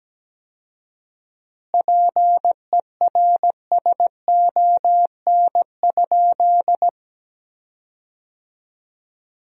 Morse code, heard as a keyed tone, about 17 wpm, 700 Hz. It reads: PERSON?
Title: 4x Word-Spacing - Sentences from Top 400 Words - 17wpm